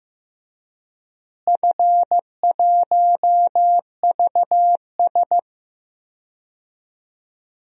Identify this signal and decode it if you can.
Morse code, heard as a keyed tone, about 15 wpm, 700 Hz. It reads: F1VS
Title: DXCC-100 Call Signs - 4 Characters - Repeated 2x - 15wpm